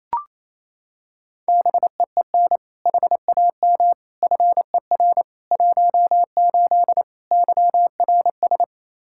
Morse code, heard as a keyed tone, about 28 wpm, 700 Hz. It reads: BEEN HAM FER 18 YRS
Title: Practice QSO - Exchange 3 - More Info - 28wpm